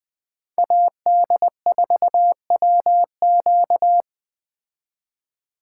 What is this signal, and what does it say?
Morse code, heard as a keyed tone, about 20 wpm, 700 Hz. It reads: AD4WQ